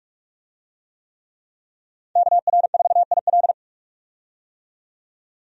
Morse code, heard as a keyed tone, about 46 wpm, 700 Hz. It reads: KR4IL
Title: USA Call Signs - 5 Characters - 45wpm